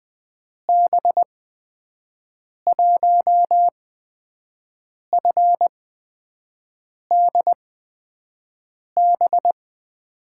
Morse code, heard as a keyed tone, about 20 wpm, spaced as for 5 wpm, 700 Hz. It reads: B1FDB
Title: USA Call Signs - 6 Characters - 8x Character Spacing - 20wpm